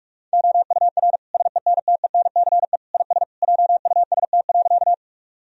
Morse code, heard as a keyed tone, about 41 wpm, 700 Hz. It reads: OUR SENTENCE IS JUST.
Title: Sentences from Top 1000 Words - Review Part 1 - 45wpm